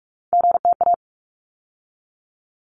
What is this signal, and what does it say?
Morse code, heard as a keyed tone, about 46 wpm, 700 Hz. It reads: GTA
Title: ARRL Sections 45wpm